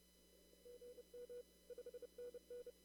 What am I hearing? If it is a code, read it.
Morse code, heard as a keyed tone, about 30 wpm, 490 Hz. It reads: GM 5NN
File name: CW / kt5j wpx cw-op-k5pi